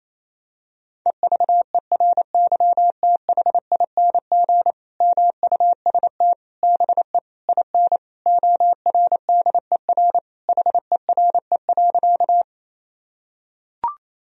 Morse code, heard as a keyed tone, about 28 wpm, 700 Hz. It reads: EVERYTHING MUST BE IN ORDER HERE.
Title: Sentences from Top 1000 Words - Review Part 2 - 28wpm